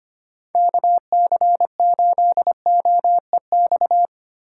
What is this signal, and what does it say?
Morse code, heard as a keyed tone, about 25 wpm, 700 Hz. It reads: KC8OEX
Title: USA Call Signs - 6 Characters - 4x Character Spacing - 25wpm